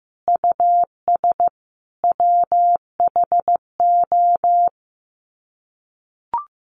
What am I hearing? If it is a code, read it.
Morse code, heard as a keyed tone, about 15 wpm, 700 Hz. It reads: US WHO
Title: Sets of 5 Words Encore 15wpm